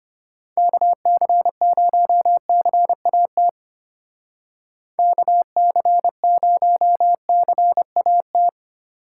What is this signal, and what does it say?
Morse code, heard as a keyed tone, about 34 wpm, 700 Hz. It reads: KC0CAT KC0CAT